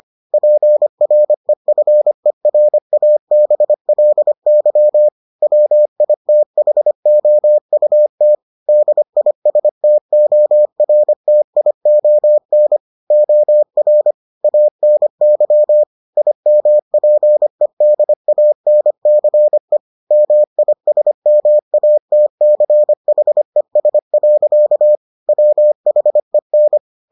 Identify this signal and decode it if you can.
Morse code, heard as a keyed tone, about 25 wpm, 600 Hz. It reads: PREFERABLY WITHOUT DISTORTION OR ANY IMPEDANCE MISMATCHES. WHEN